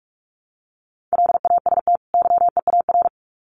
Morse code, heard as a keyed tone, about 45 wpm, 700 Hz. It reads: LAST YEAR
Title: Sets of 2 Words 45wpm